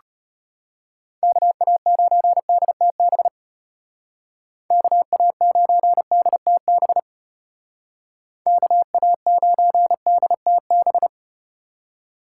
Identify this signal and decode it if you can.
Morse code, heard as a keyed tone, about 33 wpm, 700 Hz. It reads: KA9DTB KA9DTB KA9DTB